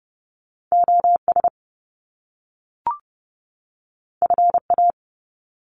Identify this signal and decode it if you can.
Morse code, heard as a keyed tone, about 30 wpm, 700 Hz. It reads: OS FA
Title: Top Two Letter Combinations in Words 30wpm